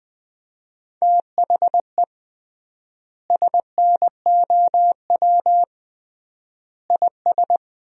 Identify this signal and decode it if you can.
Morse code, heard as a keyed tone, about 20 wpm, 700 Hz. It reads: THE SNOW IS
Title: 3x Word-Spacing - Sentences from Top 500 Words - 20wpm